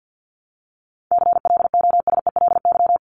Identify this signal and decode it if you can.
Morse code, heard as a keyed tone, about 50 wpm, 700 Hz. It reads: CLOSELY
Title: Top 1251-1500 Words 50wpm